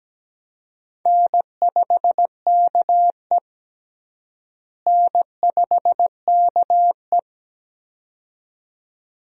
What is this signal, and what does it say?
Morse code, heard as a keyed tone, about 17 wpm, 700 Hz. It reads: N5KE N5KE
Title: USA Call Signs - 4 Characters - Repeated 2x - 17wpm